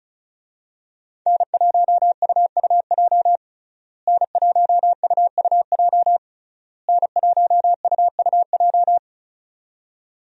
Morse code, heard as a keyed tone, about 36 wpm, 700 Hz. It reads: N1UUJ N1UUJ N1UUJ
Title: USA Call Signs - 5 Characters - Repeated 3x - 35wpm